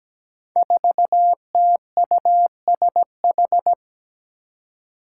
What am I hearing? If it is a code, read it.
Morse code, heard as a keyed tone, about 17 wpm, 700 Hz. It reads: 4TUSH
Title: Five Challenging Letter-Number Combinations 17wpm